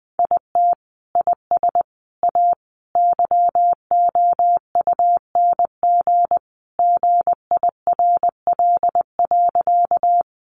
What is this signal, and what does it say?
Morse code, heard as a keyed tone, about 20 wpm, 700 Hz. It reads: IT IS A YOUNG GIRL.
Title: Sentences from Top 400 Words 20wpm